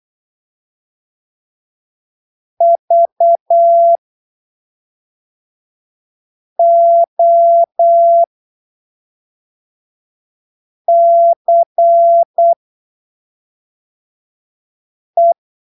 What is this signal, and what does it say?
Morse code, heard as a keyed tone, about 8 wpm, spaced as for 3 wpm, 680 Hz. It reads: VOCE